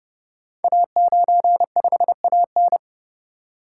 Morse code, heard as a keyed tone, about 30 wpm, 700 Hz. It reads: A95AN
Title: Code Groups - 5 Letter:Numbers - 30wpm